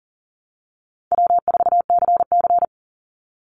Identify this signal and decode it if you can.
Morse code, heard as a keyed tone, about 40 wpm, 700 Hz. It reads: W4CC
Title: Easy Call Signs Repeated 2x - 40wpm